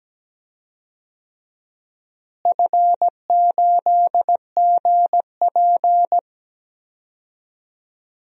Morse code, heard as a keyed tone, about 17 wpm, 700 Hz. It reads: F8GP
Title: DXCC-100 Call Signs - 4 Characters - Repeated 2x - 17wpm